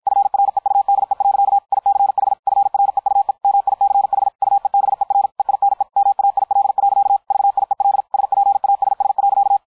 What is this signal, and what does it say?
Morse code, heard as a keyed tone, about 53 wpm, 800 Hz. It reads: FREUDE. ICH FREUE MICH UEBER EINE MAIL. VIEL SPASS.
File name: dk5ke / Beispiel 250BpM